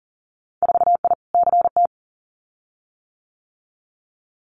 Morse code, heard as a keyed tone, about 38 wpm, 700 Hz. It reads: 4I CT
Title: Winter Field Day Exchanges Encore 40wpm